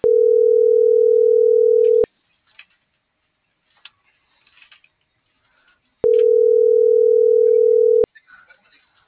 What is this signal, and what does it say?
Ringback tone (North America), 440 + 480 Hz.